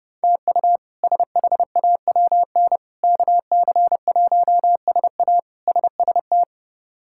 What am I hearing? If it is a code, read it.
Morse code, heard as a keyed tone, about 30 wpm, 700 Hz. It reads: TU SHAWN KC1SA SST
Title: K1USN SST QSO Practice - 30wpm